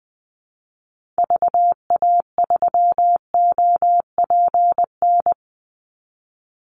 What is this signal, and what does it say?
Morse code, heard as a keyed tone, about 20 wpm, 700 Hz. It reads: VA3OPN
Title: Call Signs Repeated 3x - 20wpm